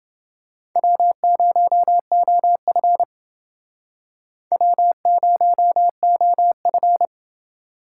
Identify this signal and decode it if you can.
Morse code, heard as a keyed tone, about 29 wpm, 700 Hz. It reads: W0OF W0OF